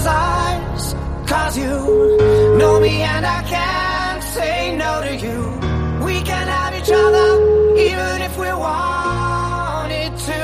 Ringback tone (Europe, 425 Hz), with noise.